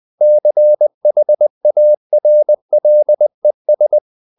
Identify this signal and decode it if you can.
Morse code, heard as a keyed tone, about 20 wpm, 600 Hz. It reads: CHARLES
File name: CW / Practice / EnglishQuotations0320